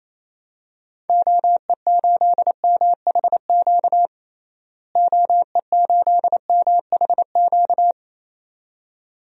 Morse code, heard as a keyed tone, about 28 wpm, 700 Hz. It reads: OE8MHQ OE8MHQ
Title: DXCC-100 Call Signs - 6 Characters - Repeated 2x - 28wpm